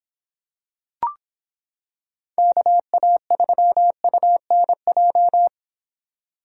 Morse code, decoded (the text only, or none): KA3UNJ